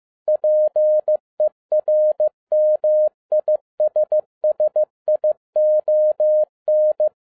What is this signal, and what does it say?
Morse code, heard as a keyed tone, about 15 wpm, 610 Hz. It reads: PERMISSION
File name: 2012-01-mp3 / n13-A NEWLY DESIGNED QSL CARD-15wpm